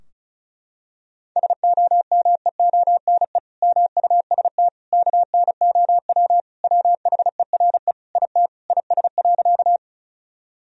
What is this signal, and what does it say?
Morse code, heard as a keyed tone, about 34 wpm, 700 Hz. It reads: SOMEONE MUST KNOW WHERE IT IS.
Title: Sentences from Top 1000 Words - Review Part 4 - 35wpm